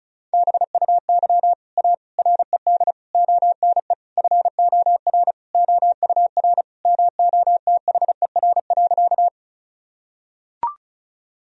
Morse code, heard as a keyed tone, about 34 wpm, 700 Hz. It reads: BUY A RED ONE FOR OUR MOTHER.
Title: Sentences from Top 1000 Words - Review Part 2 - 35wpm